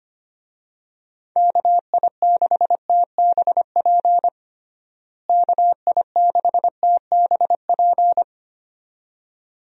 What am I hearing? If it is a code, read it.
Morse code, heard as a keyed tone, about 25 wpm, 700 Hz. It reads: KI6TBP KI6TBP
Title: USA Call Signs - 6 Characters - Repeated 2x - 25wpm